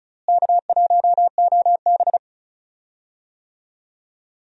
Morse code, heard as a keyed tone, about 36 wpm, 700 Hz. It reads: K1OB
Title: USA Call Signs - 4 Characters - Repeated 2x - 35wpm